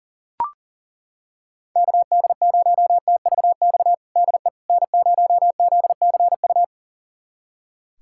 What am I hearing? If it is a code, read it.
Morse code, heard as a keyed tone, about 40 wpm, 700 Hz. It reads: KD0TVX DE N0ZCU
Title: USA Callsign DE Callsign 40wpm